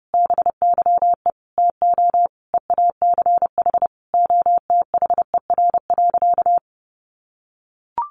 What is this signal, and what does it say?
Morse code, heard as a keyed tone, about 30 wpm, 700 Hz. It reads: BYE TO EACH OTHER.
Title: Easy Phrases 30wpm